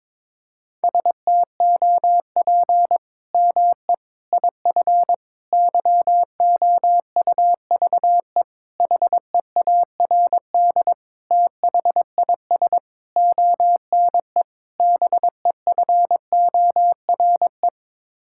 Morse code, heard as a keyed tone, about 22 wpm, 700 Hz. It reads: STOP ME IF YOUVE HEARD THIS ONE BEFORE
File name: practice / 22wpm / Sets of 8 Words - 22wpm